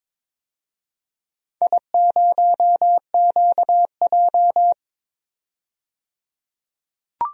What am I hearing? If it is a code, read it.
Morse code, heard as a keyed tone, about 22 wpm, 700 Hz. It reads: I0QJ
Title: DXCC-100 Call Signs - 4 Characters - 22wpm